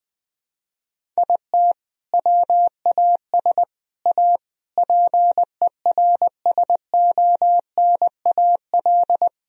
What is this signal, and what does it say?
Morse code, heard as a keyed tone, about 20 wpm, 700 Hz. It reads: IT WAS A PERSONAL
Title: Sentences from Top 1000 Words - Review Part 4 - 20wpm